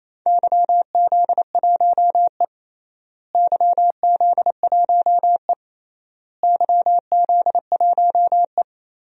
Morse code, heard as a keyed tone, about 28 wpm, 700 Hz. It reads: YZ1E YZ1E YZ1E
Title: DXCC-100 Call Signs - 4 Characters - Repeated 3x - 28wpm